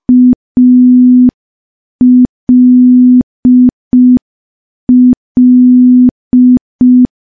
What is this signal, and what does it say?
Morse code, heard as a keyed tone, about 5 wpm, 250 Hz. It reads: ALL
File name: morse / morse-training-05WPM